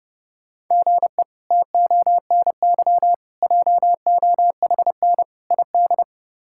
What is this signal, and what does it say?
Morse code, heard as a keyed tone, about 30 wpm, 700 Hz. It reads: GE TONY JOHN ID
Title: K1USN SST QSO Practice - 30wpm